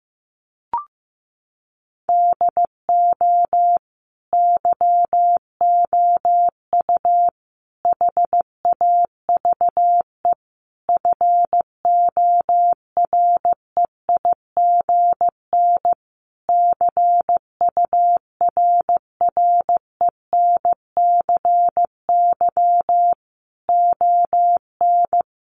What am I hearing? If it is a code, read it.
Morse code, heard as a keyed tone, about 15 wpm, 700 Hz. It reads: DO YOU HAVE FOREIGN CURRENCY ON
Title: Sentences from Top 1500 Words 15wpm